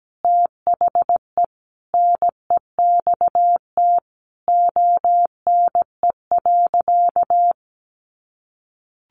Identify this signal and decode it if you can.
Morse code, heard as a keyed tone, about 17 wpm, 700 Hz. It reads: THE NEXT ONE.
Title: Sentences from Top 400 Words 17wpm